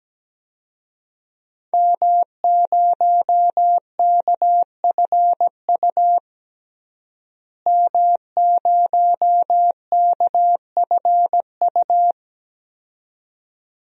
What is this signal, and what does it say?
Morse code, heard as a keyed tone, about 17 wpm, 700 Hz. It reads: M0KFU M0KFU
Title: DXCC-100 Call Signs - 5 Characters - Repeated 2x - 17wpm